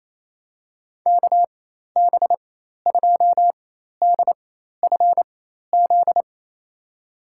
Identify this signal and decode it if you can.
Morse code, heard as a keyed tone, about 28 wpm, spaced as for 13 wpm, 700 Hz. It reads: KB2DFZ
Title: USA Call Signs - 6 Characters - 4x Character Spacing - 28wpm